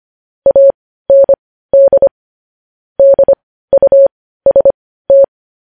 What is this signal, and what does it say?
Morse code, heard as a keyed tone, about 25 wpm, spaced as for 15 wpm, 550 Hz. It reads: AND DUST
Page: news headlines in morse code at 15 wpm